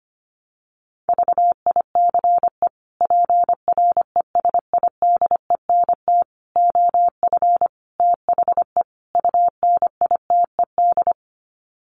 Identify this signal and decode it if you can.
Morse code, heard as a keyed tone, about 25 wpm, 700 Hz. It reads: VICE PRESIDENT OF THE UNITED
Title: Sets of 5 Words Encore2 25wpm